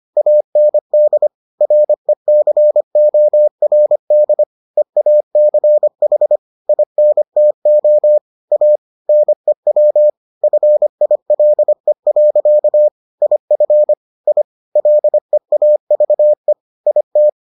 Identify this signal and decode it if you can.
Morse code, heard as a keyed tone, about 25 wpm, 600 Hz. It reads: AND RECORD EACH INTO A NEW FILE. IF I LEAVE IT